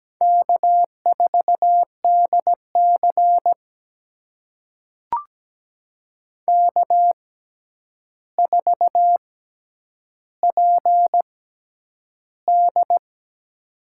Morse code, heard as a keyed tone, about 17 wpm, spaced as for 9 wpm, 700 Hz. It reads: K4DC K4PD
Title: USA Call Signs - 4 Characters - 6x Character Spacing - 17wpm